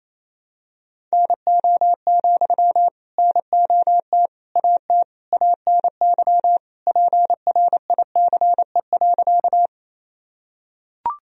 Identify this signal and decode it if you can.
Morse code, heard as a keyed tone, about 27 wpm, 700 Hz. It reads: NO, NOT AT ANY PRICE.
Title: Sentences from Top 1000 Words - Review Part 4 - 28wpm